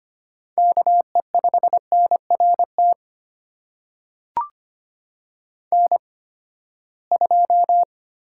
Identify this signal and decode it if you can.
Morse code, heard as a keyed tone, about 25 wpm, spaced as for 17 wpm, 700 Hz. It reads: KE5NRT N2
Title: USA Call Signs - 8x Character Spacing - 25wpm